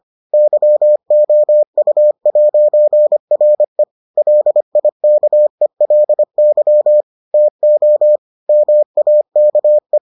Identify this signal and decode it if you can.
Morse code, heard as a keyed tone, about 25 wpm, 600 Hz. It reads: YOU'RE LIKELY TO MAKE